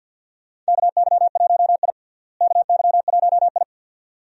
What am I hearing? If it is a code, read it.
Morse code, heard as a keyed tone, about 47 wpm, 690 Hz. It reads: KY1I KY1I
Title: USA Call Signs - 4 Characters - Repeated 2x - 50wpm